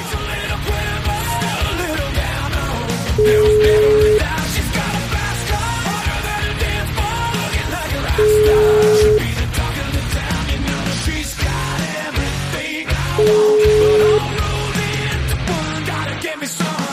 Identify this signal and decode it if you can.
Ringback tone (Europe, 425 Hz), with noise.